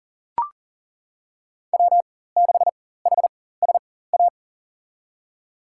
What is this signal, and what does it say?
Morse code, heard as a keyed tone, about 40 wpm, spaced as for 19 wpm, 700 Hz. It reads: W6HSA
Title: USA Call Signs - 5 Characters - 4x Character Spacing - 40wpm